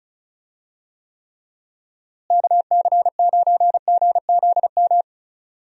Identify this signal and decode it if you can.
Morse code, heard as a keyed tone, about 36 wpm, 700 Hz. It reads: KC9GZM
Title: USA Call Signs - 6 Characters - Repeated 3x - 35wpm